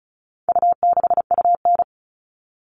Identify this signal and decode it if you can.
Morse code, heard as a keyed tone, about 34 wpm, 700 Hz. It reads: U6UN
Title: All Cut Numbers - 4-Digit 35wpm